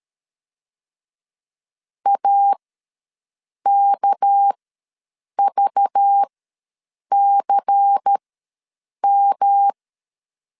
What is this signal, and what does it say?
Morse code, heard as a keyed tone, about 13 wpm, spaced as for 7 wpm, 800 Hz. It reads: AKVCM
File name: LEZIONI / 6 PKX0 / 6 PK ws 12wpm 3116